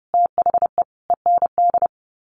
Morse code, heard as a keyed tone, about 30 wpm, 700 Hz. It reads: THE END
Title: Sets of 5 Words Encore2 30wpm